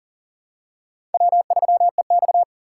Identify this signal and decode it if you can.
Morse code, heard as a keyed tone, about 44 wpm, 700 Hz. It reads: W3EX
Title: USA Call Signs - 4 Characters - 8x Character Spacing - 40wpm